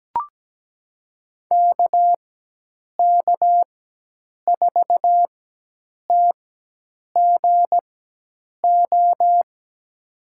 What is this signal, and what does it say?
Morse code, heard as a keyed tone, about 17 wpm, spaced as for 8 wpm, 700 Hz. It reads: KK4TGO
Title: USA Call Signs - 6 Characters - 4x Character Spacing - 17wpm